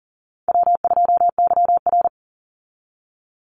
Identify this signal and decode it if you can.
Morse code, heard as a keyed tone, about 40 wpm, 700 Hz. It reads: W2YR
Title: Call Signs Repeated 3x - 40wpm